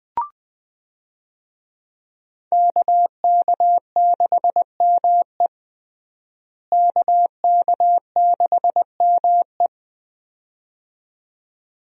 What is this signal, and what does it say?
Morse code, heard as a keyed tone, about 20 wpm, 700 Hz. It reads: KK6ME KK6ME
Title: USA Call Signs - 5 Characters - Repeated 2x - 20wpm